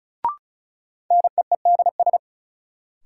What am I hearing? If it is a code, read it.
Morse code, heard as a keyed tone, about 36 wpm, 700 Hz. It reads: NEEDS
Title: Sets of 3 Words - New Word Drill - 35wpm